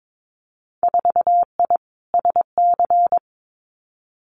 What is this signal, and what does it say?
Morse code, heard as a keyed tone, about 22 wpm, 700 Hz. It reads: 4I SC
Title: Winter Field Day Exchanges 22wpm